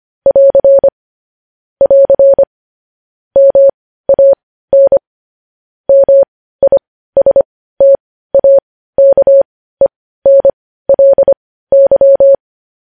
Morse code, heard as a keyed tone, about 25 wpm, spaced as for 15 wpm, 550 Hz. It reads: <AR> <AR> MAN MISTAKENLY
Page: news-headlines-in-morse-code-at-15-wpm